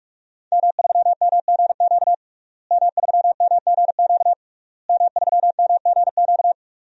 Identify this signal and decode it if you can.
Morse code, heard as a keyed tone, about 46 wpm, 700 Hz. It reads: M3MGQ M3MGQ M3MGQ
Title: DXCC-100 Call Signs - 5 Characters - Repeated 3x - 45wpm